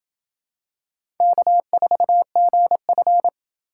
Morse code, heard as a keyed tone, about 24 wpm, 700 Hz. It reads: K4GF
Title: Speed Racing - USA 1x2 Call Signs 20wpm